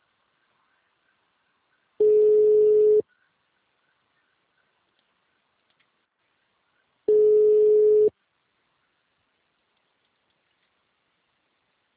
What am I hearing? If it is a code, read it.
Ringback tone (Europe, 425 Hz).